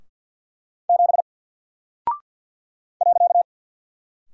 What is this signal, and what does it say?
Morse code, heard as a keyed tone, about 48 wpm, 700 Hz. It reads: 7 .
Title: Single Character (Letter:Number:Punctuation:Prosign) Alt 50wpm